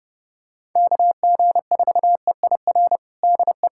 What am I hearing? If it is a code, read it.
Morse code, heard as a keyed tone, about 30 wpm, 700 Hz. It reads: KG4EIR DE
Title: USA Callsign DE Callsign 30wpm